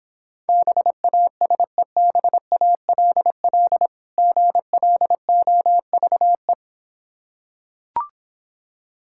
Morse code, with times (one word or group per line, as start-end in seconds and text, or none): 0.5-3.9 BASEBALL
4.2-6.5 GLOVE